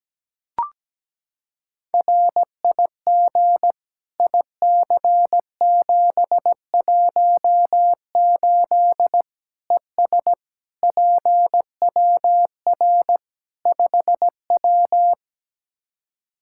Morse code, heard as a keyed tone, about 17 wpm, 700 Hz. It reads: RIG IC718 ES PWR 5W
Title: Practice QSO - Exchange 2 - Info - 17wpm